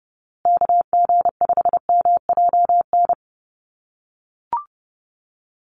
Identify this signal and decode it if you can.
Morse code, heard as a keyed tone, about 30 wpm, 700 Hz. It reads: KG5MJN
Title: USA Call Signs Repeated 2x 30wpm